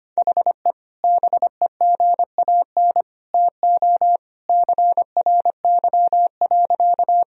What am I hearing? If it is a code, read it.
Morse code, heard as a keyed tone, about 25 wpm, 700 Hz. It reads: HE BEGAN TO CRY.
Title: Sentences from Top 1000 Words - Review Part 2 - 25wpm